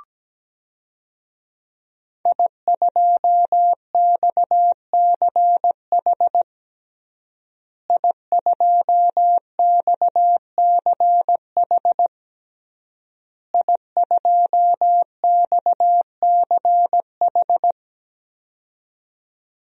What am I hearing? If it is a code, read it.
Morse code, heard as a keyed tone, about 17 wpm, 700 Hz. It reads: I2XCH I2XCH I2XCH
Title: DXCC-100 Call Signs - 5 Characters - Repeated 3x - 17wpm